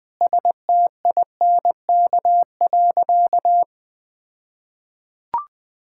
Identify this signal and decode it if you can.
Morse code, heard as a keyed tone, about 20 wpm, 700 Hz. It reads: STINK.
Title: All-Age Appropriate Jokes - 20wpm